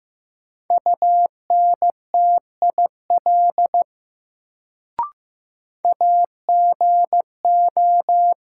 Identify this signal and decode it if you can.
Morse code, heard as a keyed tone, about 15 wpm, 700 Hz. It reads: UNTIL AGO
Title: Sentences from Top 500 Words - New Word Drill - 15wpm